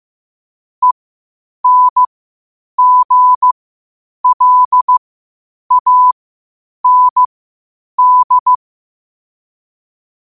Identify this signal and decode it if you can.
Morse code, heard as a keyed tone, about 15 wpm, spaced as for 8 wpm, 1000 Hz. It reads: ENGLAND
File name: qotdfiles / 8wpm